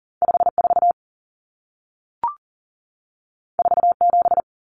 Morse code, heard as a keyed tone, about 40 wpm, 700 Hz. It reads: 54 47
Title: Numbers - 2-Digit 40wpm